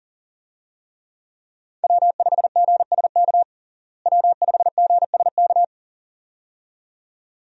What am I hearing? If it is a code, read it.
Morse code, heard as a keyed tone, about 40 wpm, 700 Hz. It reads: W5GSK W5GSK
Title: USA Call Signs - 5 Characters - Repeated 2x - 40wpm